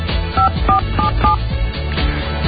DTMF, heard as telephone touch tones, tones of about 120 ms. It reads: 5477